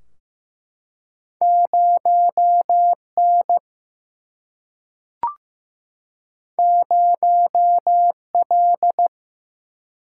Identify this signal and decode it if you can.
Morse code, heard as a keyed tone, about 15 wpm, 700 Hz. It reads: 0 N 0 L
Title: Single-Letter Single-Number 15wpm